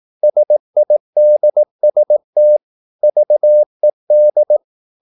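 Morse code, heard as a keyed tone, about 18 wpm, 600 Hz. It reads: SIDST VED
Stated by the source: Slow (QRS) Morse practice at 18 wpm, CW med Gnister.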